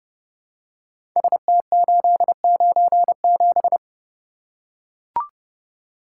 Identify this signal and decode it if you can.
Morse code, heard as a keyed tone, about 28 wpm, 700 Hz. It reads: ST897
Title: Code Groups - 5 Letter:Numbers - 30wpm